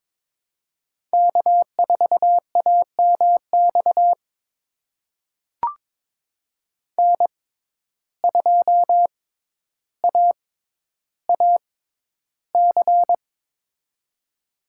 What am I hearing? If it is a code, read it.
Morse code, heard as a keyed tone, about 22 wpm, spaced as for 11 wpm, 700 Hz. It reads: K4AMX N2AAC
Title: USA Call Signs - 5 Characters - 6x Character Spacing - 22wpm